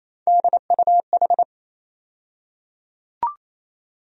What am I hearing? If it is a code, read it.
Morse code, heard as a keyed tone, about 28 wpm, 700 Hz. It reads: DUH
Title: USA Call Sign Postfixes - 3 Letters 28wpm